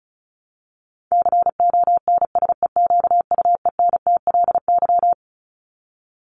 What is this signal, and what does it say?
Morse code, heard as a keyed tone, about 34 wpm, 700 Hz. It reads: CONSEQUENTLY
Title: LY Suffix - Top 100 Words with Base Words 35wpm